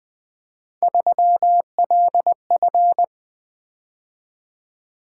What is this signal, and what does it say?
Morse code, heard as a keyed tone, about 24 wpm, 700 Hz. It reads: 3LF
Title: Speed-Racing - Three Letter-Number Combinations - 20wpm